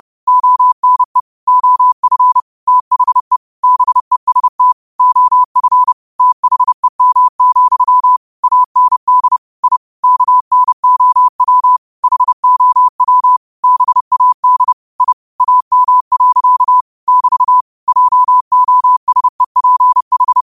Morse code, heard as a keyed tone, about 30 wpm, 1000 Hz. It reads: ONE OF THE BEST OF THEM, AND I KNOW HOW BAD I AM. <BT> JOSEPH